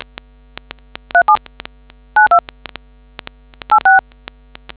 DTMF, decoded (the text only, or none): 3792 86